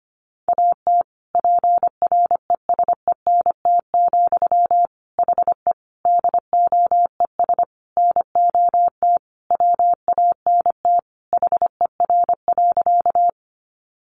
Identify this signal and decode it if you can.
Morse code, heard as a keyed tone, about 25 wpm, 700 Hz. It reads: AT PRESENT, HE DOES NOT WANT HER.
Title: Sentences from Top 800 Words 25wpm